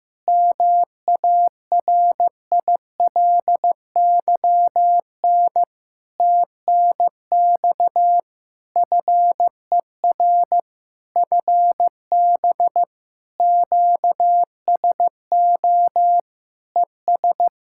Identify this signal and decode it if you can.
Morse code, heard as a keyed tone, about 16 wpm, 700 Hz. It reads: MARILYN TNX FER FB QSO ES
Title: Practice QSO - Exchange 4 - Ending - 15wpm